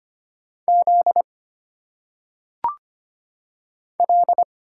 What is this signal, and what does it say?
Morse code, heard as a keyed tone, about 25 wpm, 700 Hz. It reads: Z L
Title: Single Letters - Phonetic Pronunciation 25wpm